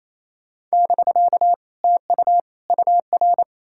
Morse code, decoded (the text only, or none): <BK> TU UR